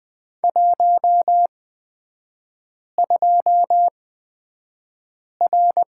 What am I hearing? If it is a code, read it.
Morse code, heard as a keyed tone, about 20 wpm, spaced as for 5 wpm, 690 Hz. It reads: 12R
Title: 229 Lesson 37e - Letter x - Three Characters 20wpm